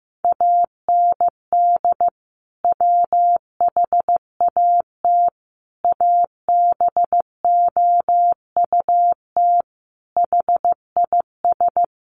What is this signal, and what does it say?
Morse code, heard as a keyed tone, about 14 wpm, 700 Hz. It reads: AND WHAT ABOUT HIS
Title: Sentences from Top 1200 Words 15wpm